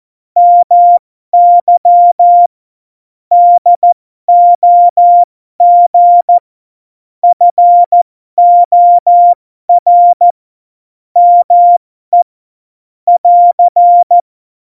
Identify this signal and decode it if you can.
Morse code, heard as a keyed tone, about 14 wpm, spaced as for 12 wpm, 700 Hz. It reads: MY DOG FOR ME <AR>
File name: Morsekurs / Witze